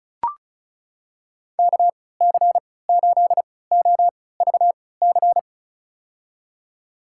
Morse code, heard as a keyed tone, about 36 wpm, spaced as for 20 wpm, 700 Hz. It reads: KC8OVC
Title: USA Call Signs - 6 Characters - 3x Character Spacing - 35wpm